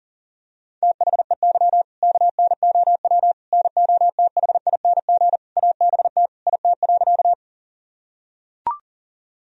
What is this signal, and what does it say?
Morse code, heard as a keyed tone, about 40 wpm, 700 Hz. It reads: THEY KNOW NOTHING ABT IT.
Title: Sentences from Top 500 Words and Abbreviations 40wpm